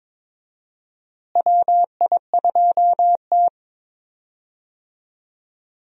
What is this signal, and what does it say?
Morse code, heard as a keyed tone, about 22 wpm, 700 Hz. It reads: WI2T
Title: USA Call Signs - 4 Characters - Repeated 3x - 22wpm